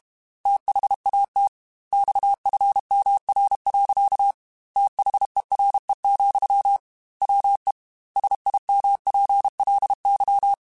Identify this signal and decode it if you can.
Morse code, heard as a keyed tone, about 32 wpm, 800 Hz. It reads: THAT XFMR. THERE, WE SIMPLY
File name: Technical Subjects / Technical Subjects-mp3s / Sw Pwr Sup @30